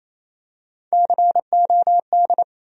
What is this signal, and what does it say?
Morse code, heard as a keyed tone, about 28 wpm, 700 Hz. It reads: COD
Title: USA Call Sign Postfixes - 3 Letters 28wpm